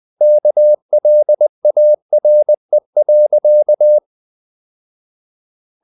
Morse code, heard as a keyed tone, about 20 wpm, 600 Hz. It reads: KLARE.
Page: Ferie 20 wpm